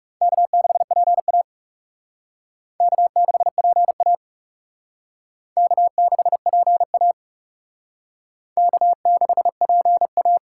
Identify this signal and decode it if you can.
Morse code, heard as a keyed tone, about 39 wpm, 700 Hz. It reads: K6PA K6PA K6PA K6PA